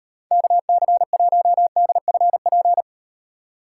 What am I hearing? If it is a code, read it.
Morse code, heard as a keyed tone, about 36 wpm, 700 Hz. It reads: KC1DFP